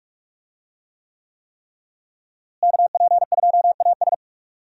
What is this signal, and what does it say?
Morse code, heard as a keyed tone, about 41 wpm, 700 Hz. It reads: KP2AS
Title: USA Call Signs - 5 Characters - Repeated 3x - 45wpm